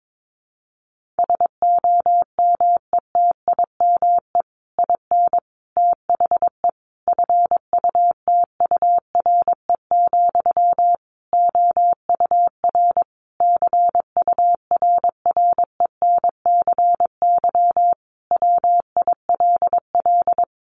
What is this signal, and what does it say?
Morse code, heard as a keyed tone, about 22 wpm, 700 Hz. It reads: SOMETIME IN THE FUTURE, OUR CURRENCY WILL
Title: Sentences from Top 1500 Words 22wpm